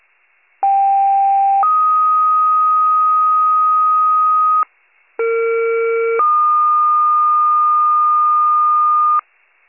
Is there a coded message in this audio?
Yes: two-tone page from 0.6 to 9.2 s, heard as a one-second tone, then a three-second tone, about 790 then 1250 Hz, about 460 then 1220 Hz.